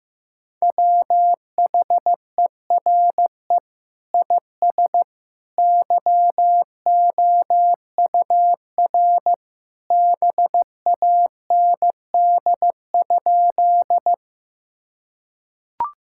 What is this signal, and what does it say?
Morse code, heard as a keyed tone, about 15 wpm, 700 Hz. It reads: WHERE IS YOUR BAND?
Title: Sentences from Top 1600 Words 15wpm